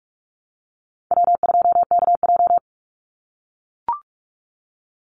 Morse code, heard as a keyed tone, about 44 wpm, 700 Hz. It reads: W2KJ
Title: Call Signs - 45wpm